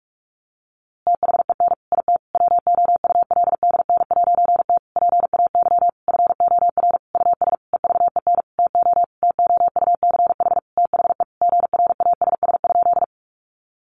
Morse code, heard as a keyed tone, about 44 wpm, 700 Hz. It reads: THEN IT WOULDN'T PAY FOR US EVEN TO TOUCH THE GRASS?
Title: A Sound Of Thunder 45wpm